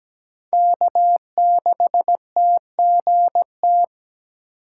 Morse code, heard as a keyed tone, about 17 wpm, 700 Hz. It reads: K6TGT
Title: USA Call Signs - 3x Character Spacing - 17wpm